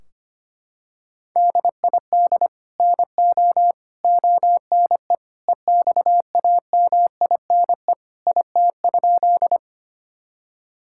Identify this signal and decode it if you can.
Morse code, heard as a keyed tone, about 25 wpm, 700 Hz. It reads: DID NO ONE EXAMINE IT?
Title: Sentences from Top 1600 Words 25wpm